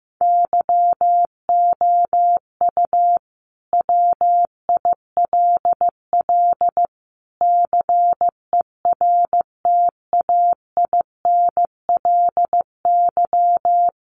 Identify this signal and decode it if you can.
Morse code, heard as a keyed tone, about 14 wpm, 700 Hz. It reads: YOU WILL CERTAINLY
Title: Sentences from Top 1400 Words 15wpm